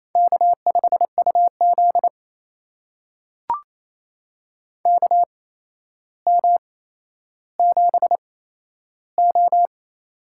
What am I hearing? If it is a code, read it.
Morse code, heard as a keyed tone, about 28 wpm, spaced as for 12 wpm, 700 Hz. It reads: K5UZ KM7O
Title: USA Call Signs - 4 Characters - 8x Character Spacing - 28wpm